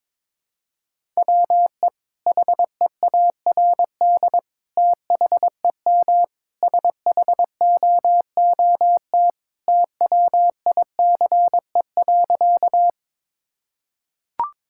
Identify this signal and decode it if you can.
Morse code, heard as a keyed tone, about 22 wpm, 700 Hz. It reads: WE HEARD THEM SHOOT TWICE.
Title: Sentences from Top 1600 Words 22wpm